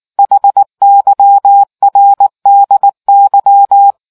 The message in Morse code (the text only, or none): HYRDY